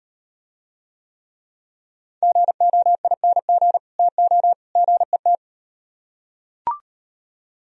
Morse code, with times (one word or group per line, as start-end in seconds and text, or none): 2.2-3.8 GOING
4.0-4.5 TO
4.8-5.4 GET